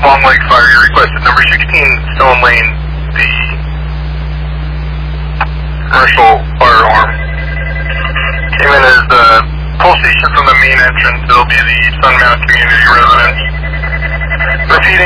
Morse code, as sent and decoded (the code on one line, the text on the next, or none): none